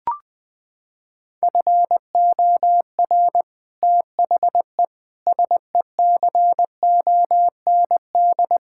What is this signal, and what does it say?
Morse code, heard as a keyed tone, about 20 wpm, 700 Hz. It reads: FOR THE SECOND
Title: Sentences with 8-Word Phrases - 20wpm